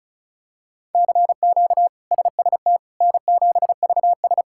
Morse code, heard as a keyed tone, about 34 wpm, 700 Hz. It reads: CQ SST N7VS
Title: K1USN SST QSO Practice - 35wpm